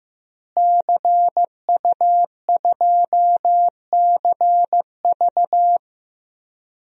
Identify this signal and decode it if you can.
Morse code, heard as a keyed tone, about 15 wpm, 700 Hz. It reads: CU2CV
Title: DXCC-100 Call Signs - 5 Characters - Repeated 3x - 15wpm